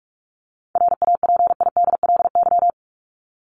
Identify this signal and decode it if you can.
Morse code, heard as a keyed tone, about 44 wpm, 700 Hz. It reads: RAPIDLY
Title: LY Suffix - Top 100 Words with Base Words 45wpm